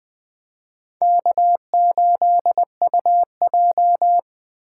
Morse code, heard as a keyed tone, about 21 wpm, 700 Hz. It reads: K8UJ
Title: Speed Racing - USA 1x2 Call Signs 17wpm